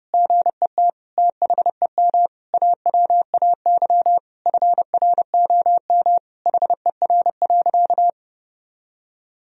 Morse code, heard as a keyed tone, about 32 wpm, 700 Hz. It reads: GET THEM AWAY FROM HER.
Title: Sentences from Top 1000 Words - Review Part 2 - 30wpm